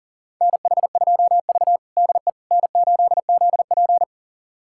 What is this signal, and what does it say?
Morse code, heard as a keyed tone, about 40 wpm, 700 Hz. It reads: NH2V DE N8ZP
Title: USA Callsign DE Callsign - 4 Characters - 40wpm